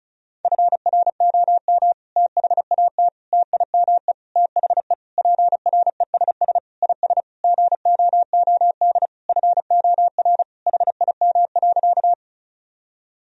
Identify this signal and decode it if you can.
Morse code, heard as a keyed tone, about 36 wpm, 700 Hz. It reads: FROM THAT TIME THE PRESS IS GOOD FOR HIM.
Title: Sentences from Top 1000 Words - Review Part 1 - 35wpm